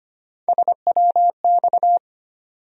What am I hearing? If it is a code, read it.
Morse code, heard as a keyed tone, about 25 wpm, 700 Hz. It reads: SWX
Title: USA Call Sign Postfixes - 3 Letters 25wpm